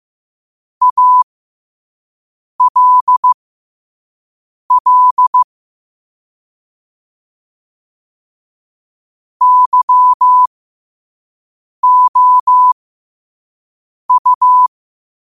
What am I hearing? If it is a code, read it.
Morse code, heard as a keyed tone, about 15 wpm, spaced as for 5 wpm, 1000 Hz. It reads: ALL YOU
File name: qotdfiles / 5wpm